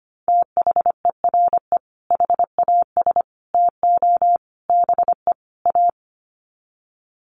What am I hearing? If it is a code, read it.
Morse code, heard as a keyed tone, about 25 wpm, 700 Hz. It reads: THERE HAS TO BE A
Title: Sets of 5 Words Encore 25wpm